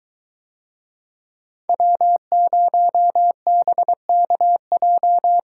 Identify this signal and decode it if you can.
Morse code, heard as a keyed tone, about 23 wpm, 700 Hz. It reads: W0BKJ